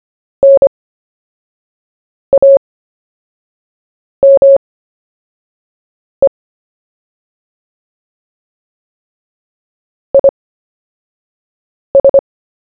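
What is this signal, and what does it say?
Morse code, heard as a keyed tone, about 25 wpm, spaced as for 5 wpm, 550 Hz. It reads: NAME IS
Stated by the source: CW story-telling practice, Short Story 1 sent at 5 wpm, by VK4JE, Queensland, Australia.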